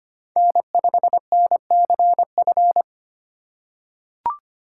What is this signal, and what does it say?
Morse code, heard as a keyed tone, about 25 wpm, 700 Hz. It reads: N5NCF
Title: USA Call Signs - 5 Characters - 2x Character Spacing - 25wpm